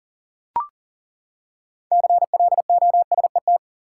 Morse code, heard as a keyed tone, about 40 wpm, 700 Hz. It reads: CLOSET
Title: Six Letter Words - 40wpm